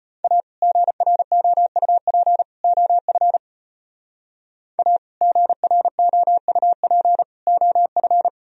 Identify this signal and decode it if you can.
Morse code, heard as a keyed tone, about 30 wpm, 700 Hz. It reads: A GROUP OF A GROUP OF
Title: Speed-Racing - Sets of 3 Words 25wpm